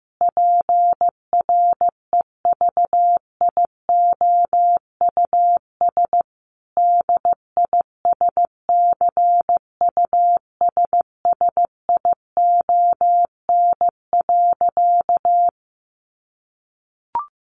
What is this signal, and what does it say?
Morse code, heard as a keyed tone, about 15 wpm, 700 Hz. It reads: PREVIOUS DISCUSSION.
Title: Sentences from Top 1500 Words 15wpm